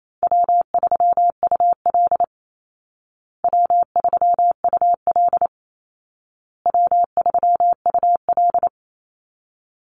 Morse code, heard as a keyed tone, about 28 wpm, 700 Hz. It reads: W3UL W3UL W3UL
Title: Easy Call Signs Repeated 3x - 28wpm